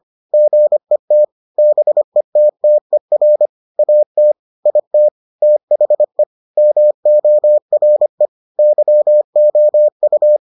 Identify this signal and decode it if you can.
Morse code, heard as a keyed tone, about 25 wpm, 600 Hz. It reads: GET BETTER AT IT THE MORE YOU